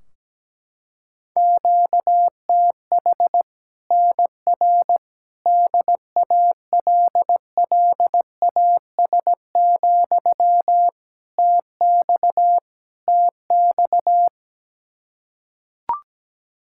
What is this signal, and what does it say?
Morse code, heard as a keyed tone, about 17 wpm, 700 Hz. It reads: QTH NR DALLAS, TX TX
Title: Practice QSO - Exchange 1 - Beginning - 17wpm